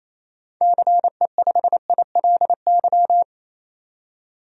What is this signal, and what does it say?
Morse code, heard as a keyed tone, about 28 wpm, 700 Hz. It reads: CE5ILY
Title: DXCC-100 Call Signs - 6 Characters - 28wpm